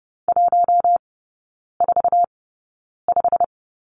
Morse code, heard as a keyed tone, about 30 wpm, spaced as for 9 wpm, 700 Hz. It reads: 145
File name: course / 30:10wpm / 087 Lesson 15e - Number 5 - Three Characters 30wpm